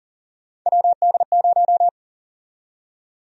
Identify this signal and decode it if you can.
Morse code, heard as a keyed tone, about 40 wpm, 700 Hz. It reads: WD0
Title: USA Call Sign Prefixes - Repeated 2x - 40wpm